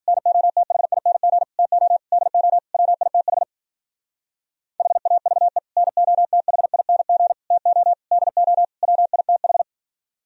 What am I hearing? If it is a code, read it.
Morse code, heard as a keyed tone, about 51 wpm, 700 Hz. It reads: NOTHING TO DO WITH HAVE NOTHING TO DO WITH